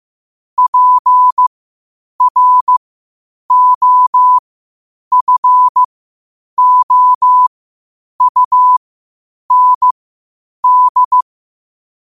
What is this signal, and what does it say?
Morse code, heard as a keyed tone, about 15 wpm, spaced as for 8 wpm, 1000 Hz. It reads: PROFOUND
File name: qotdfiles / 8wpm